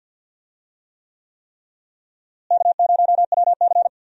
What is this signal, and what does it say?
Morse code, heard as a keyed tone, about 48 wpm, 700 Hz. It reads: K0WC
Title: USA Call Signs - 4 Characters - Repeated 3x - 50wpm